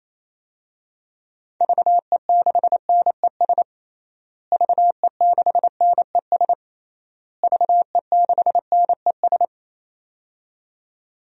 Morse code, heard as a keyed tone, about 28 wpm, 700 Hz. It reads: VE6NES VE6NES VE6NES
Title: DXCC-100 Call Signs - 6 Characters - Repeated 3x - 28wpm